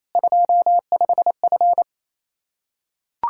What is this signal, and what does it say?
Morse code, heard as a keyed tone, about 28 wpm, 700 Hz. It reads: 25F
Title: Practice QSO - Exchange 2 - Info - 28wpm